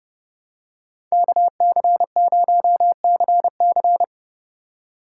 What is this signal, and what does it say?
Morse code, heard as a keyed tone, about 30 wpm, 700 Hz. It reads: KC0CC
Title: USA Call Signs - 5 Characters - 4x Character Spacing - 30wpm